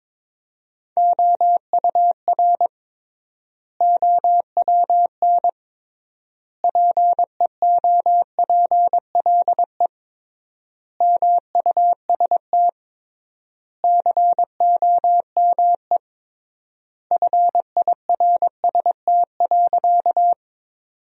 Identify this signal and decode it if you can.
Morse code, heard as a keyed tone, about 22 wpm, 700 Hz. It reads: OUR OWN PEOPLE MUST COME FIRST.
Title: 3x Word-Spacing - Sentences from Top 300 Words - 22wpm